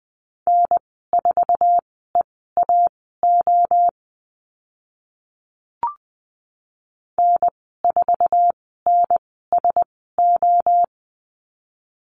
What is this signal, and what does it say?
Morse code, heard as a keyed tone, about 20 wpm, spaced as for 14 wpm, 700 Hz. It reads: N4EAO N4NSO
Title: 062 Lesson 11g - Letter h - Call Signs 20wpm